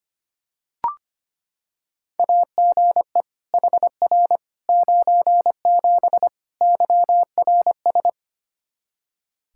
Morse code, heard as a keyed tone, about 25 wpm, 700 Hz. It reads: AGE HR 97 YRS
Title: Practice QSO - Exchange 3 - More Info - 25wpm